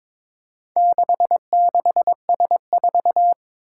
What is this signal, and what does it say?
Morse code, heard as a keyed tone, about 22 wpm, 700 Hz. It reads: 66S4
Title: Four Challenging Letter-Number Combinations 22wpm